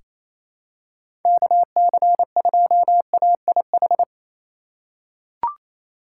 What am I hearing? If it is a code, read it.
Morse code, heard as a keyed tone, about 28 wpm, 700 Hz. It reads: KC2AIH
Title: USA Call Signs - 6 Characters - 4x Character Spacing - 28wpm